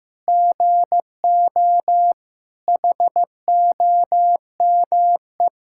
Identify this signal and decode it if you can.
Morse code, heard as a keyed tone, about 15 wpm, 700 Hz. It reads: GO HOME
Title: Sentences from Top 1000 Words - Review Part 6 - 15wpm